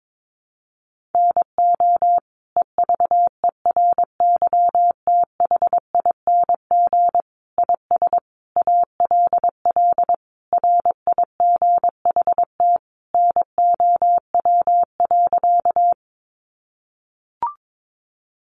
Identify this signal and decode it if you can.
Morse code, heard as a keyed tone, about 22 wpm, 700 Hz. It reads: NO EVERYTHING IS ALL RIGHT NOW.
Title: Sentences from Top 700 Words 22wpm